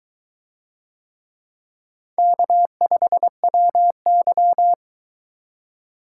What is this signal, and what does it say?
Morse code, heard as a keyed tone, about 22 wpm, 700 Hz. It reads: K5WY